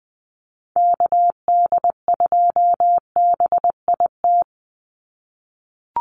Morse code, heard as a keyed tone, about 20 wpm, 700 Hz. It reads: KD2BIT
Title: USA Call Signs 20wpm